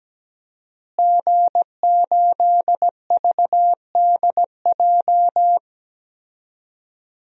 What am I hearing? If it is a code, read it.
Morse code, heard as a keyed tone, about 17 wpm, 700 Hz. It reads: G8VDJ
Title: DXCC-100 Call Signs - 5 Characters - Repeated 2x - 17wpm